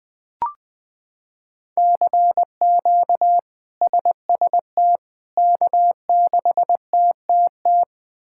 Morse code, heard as a keyed tone, about 20 wpm, 700 Hz. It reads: CQ SST K6TTT
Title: K1USN SST QSO Practice - 20wpm